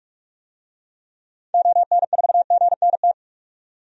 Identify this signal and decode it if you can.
Morse code, heard as a keyed tone, about 42 wpm, 700 Hz. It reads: ON4GNT
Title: DXCC-100 Call Signs - 6 Characters - 45wpm